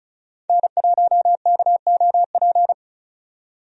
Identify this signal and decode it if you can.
Morse code, heard as a keyed tone, about 36 wpm, 700 Hz. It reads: N1KOP
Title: USA Call Signs - 5 Characters - 35wpm